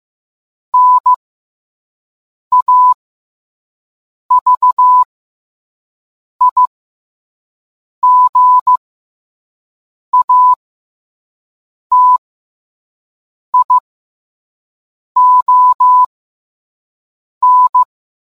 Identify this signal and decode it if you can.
Morse code, heard as a keyed tone, about 15 wpm, spaced as for 5 wpm, 1000 Hz. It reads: NAVIGATION